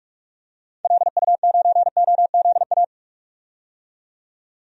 Morse code, heard as a keyed tone, about 46 wpm, 700 Hz. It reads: LU9OZA
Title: DXCC-100 Call Signs - 6 Characters - 45wpm